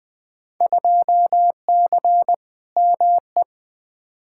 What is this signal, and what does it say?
Morse code, heard as a keyed tone, about 20 wpm, 700 Hz. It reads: 2C ME
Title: Field Day Exchanges Encore - 20wpm